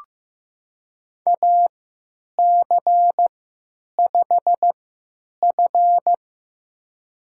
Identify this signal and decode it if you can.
Morse code, heard as a keyed tone, about 15 wpm, spaced as for 9 wpm, 700 Hz. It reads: AC5F
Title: USA Call Signs - 4 Characters - 3x Character Spacing - 15wpm